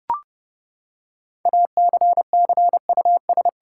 Morse code, heard as a keyed tone, about 30 wpm, 700 Hz. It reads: ACCUS
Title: Six Letter Words - 30wpm